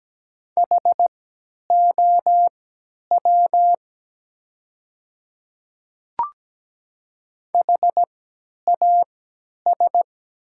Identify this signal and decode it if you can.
Morse code, heard as a keyed tone, about 17 wpm, spaced as for 10 wpm, 690 Hz. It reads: HOW HAS
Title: 3x Character Spacing - Top 100 Words - 17wpm